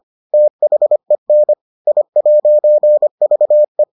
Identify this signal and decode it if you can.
Morse code, heard as a keyed tone, about 25 wpm, 600 Hz. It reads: THEN I'VE